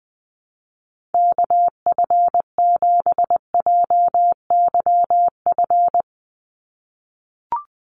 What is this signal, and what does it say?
Morse code, heard as a keyed tone, about 20 wpm, 700 Hz. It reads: KF7JYF
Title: USA Call Signs 20wpm